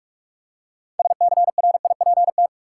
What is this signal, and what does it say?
Morse code, heard as a keyed tone, about 46 wpm, 700 Hz. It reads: SCRIPT
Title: Six Letter Words - 45wpm